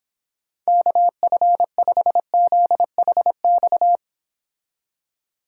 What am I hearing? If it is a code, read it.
Morse code, heard as a keyed tone, about 26 wpm, 700 Hz. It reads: KF5ZHX